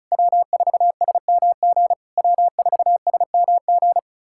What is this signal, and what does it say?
Morse code, heard as a keyed tone, about 34 wpm, 690 Hz. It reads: W4SMG W4SMG
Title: SOTA QSO Practice 35wpm